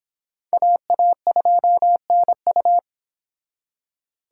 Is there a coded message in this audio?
Yes: Morse code from 0.5 to 2.8 s, heard as a keyed tone, about 29 wpm, 700 Hz. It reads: AA2NU